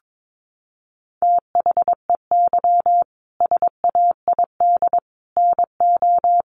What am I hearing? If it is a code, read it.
Morse code, heard as a keyed tone, about 22 wpm, 700 Hz. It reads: THEY SAID NO
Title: Sentences from Top 100 Words 22wpm